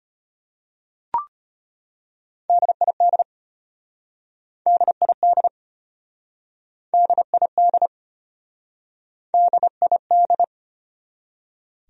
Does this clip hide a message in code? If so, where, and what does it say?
Yes: Morse code from 2.5 to 10.4 s, heard as a keyed tone, about 33 wpm, 700 Hz. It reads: DID DID DID DID